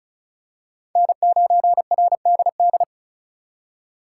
Morse code, heard as a keyed tone, about 34 wpm, 700 Hz. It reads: N9RDD
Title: USA Call Signs - 5 Characters - 3x Character Spacing - 35wpm